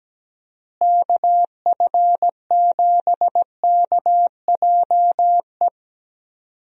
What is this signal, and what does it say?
Morse code, heard as a keyed tone, about 17 wpm, 700 Hz. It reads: KF7KJE
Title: USA Call Signs - 6 Characters - Repeated 2x - 17wpm